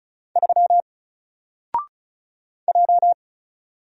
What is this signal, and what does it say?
Morse code, heard as a keyed tone, about 36 wpm, 700 Hz. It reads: 3 J
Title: Single Letter-Number Alt 35wpm